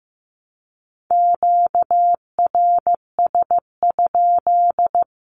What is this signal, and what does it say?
Morse code, heard as a keyed tone, about 15 wpm, 700 Hz. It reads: QRS?
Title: Q-Codes 15wpm